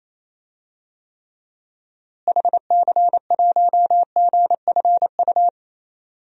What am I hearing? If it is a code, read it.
Morse code, heard as a keyed tone, about 28 wpm, 700 Hz. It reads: HC1GFU
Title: DXCC-100 Call Signs - 6 Characters - Repeated 3x - 28wpm